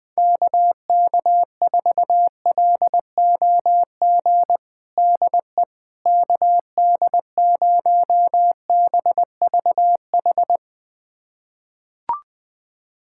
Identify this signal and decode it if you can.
Morse code, heard as a keyed tone, about 20 wpm, 700 Hz. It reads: KK4LOG DE KD0BVH
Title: USA Callsign DE Callsign - 6 Characters - 20wpm